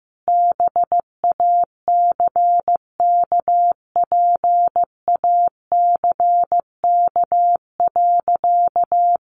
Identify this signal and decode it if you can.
Morse code, heard as a keyed tone, about 15 wpm, 700 Hz. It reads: BACKPACK.
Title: Additional Easy Phrases 15wpm